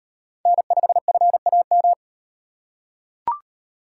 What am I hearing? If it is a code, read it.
Morse code, heard as a keyed tone, about 38 wpm, 700 Hz. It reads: N5FAM